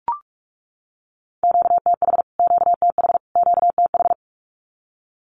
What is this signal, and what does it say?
Morse code, heard as a keyed tone, about 44 wpm, 700 Hz. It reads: QTH QTH QTH
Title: Q-Codes Repeated 3x 45wpm